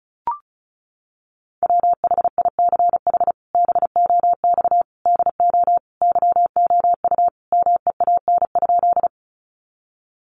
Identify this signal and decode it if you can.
Morse code, heard as a keyed tone, about 34 wpm, 700 Hz. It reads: WHICH BOX DO YOU MEAN?
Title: Sentences from Top 500 Words 35wpm